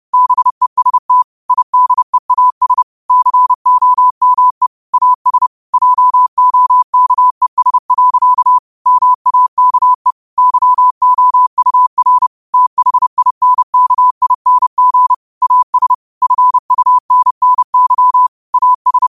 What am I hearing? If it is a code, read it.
Morse code, heard as a keyed tone, about 30 wpm, 1000 Hz. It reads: BEST IDEAS COME AS JOKES. MAKE YOUR THINKING AS FUNNY AS
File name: qotdfiles / 30wpm